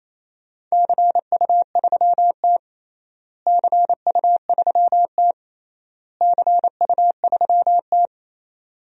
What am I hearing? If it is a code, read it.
Morse code, heard as a keyed tone, about 28 wpm, 700 Hz. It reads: CU3T CU3T CU3T
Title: DXCC-100 Call Signs - 4 Characters - Repeated 3x - 28wpm